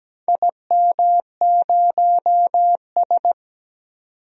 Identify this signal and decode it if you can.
Morse code, heard as a keyed tone, about 17 wpm, 700 Hz. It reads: IM0S
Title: DXCC-100 Call Signs - 4 Characters - 17wpm